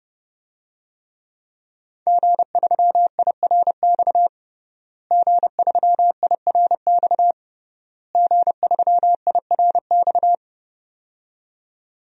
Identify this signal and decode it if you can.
Morse code, heard as a keyed tone, about 30 wpm, 700 Hz. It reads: G3IRX G3IRX G3IRX
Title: DXCC-100 Call Signs - 5 Characters - Repeated 3x - 30wpm